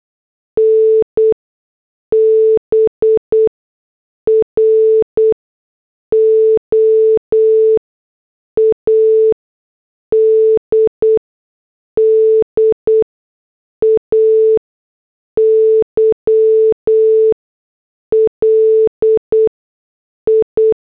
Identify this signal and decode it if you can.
Morse code, heard as a keyed tone, about 8 wpm, spaced as for 6 wpm, 440 Hz. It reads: N BROAD DAYLI
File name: morse / headlines